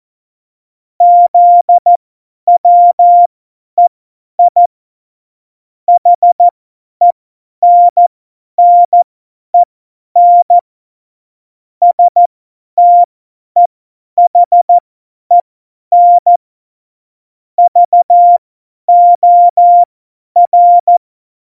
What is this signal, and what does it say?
Morse code, heard as a keyed tone, about 14 wpm, spaced as for 10 wpm, 700 Hz. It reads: ZWEI HENNEN STEHEN VOR
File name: Morsekurs / Witze / Witz 14 April 2020, 50 F70